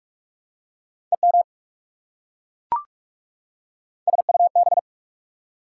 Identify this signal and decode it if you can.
Morse code, heard as a keyed tone, about 44 wpm, 690 Hz. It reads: EM SUB
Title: Prefixes and Suffixes 45wpm